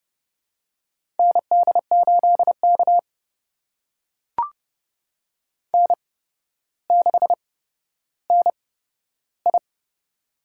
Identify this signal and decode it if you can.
Morse code, heard as a keyed tone, about 30 wpm, spaced as for 13 wpm, 700 Hz. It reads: ND8K N6NI
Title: USA Call Signs - 4 Characters - 8x Character Spacing - 30wpm